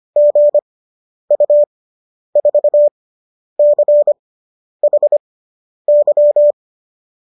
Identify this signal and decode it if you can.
Morse code, heard as a keyed tone, about 25 wpm, spaced as for 10 wpm, 570 Hz. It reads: GU4CHY